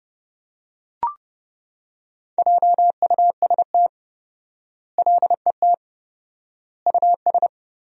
Morse code, heard as a keyed tone, about 32 wpm, 700 Hz. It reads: JUST LET US
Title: 4x Word-Spacing - Sentences from Top 500 Words - 30wpm